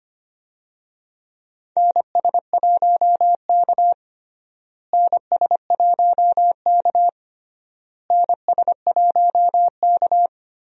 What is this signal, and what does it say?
Morse code, heard as a keyed tone, about 25 wpm, 700 Hz. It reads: NS1K NS1K NS1K
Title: USA Call Signs - 4 Characters - Repeated 3x - 25wpm